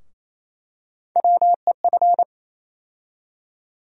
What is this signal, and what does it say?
Morse code, heard as a keyed tone, about 28 wpm, 700 Hz. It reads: WEF
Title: USA Call Sign Postfixes - 3 Letters 28wpm